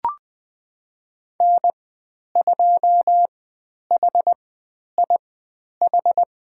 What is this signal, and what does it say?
Morse code, heard as a keyed tone, about 20 wpm, spaced as for 10 wpm, 700 Hz. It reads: N2HIH
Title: 102 Lesson 17g - Letter u - Call Signs 20wpm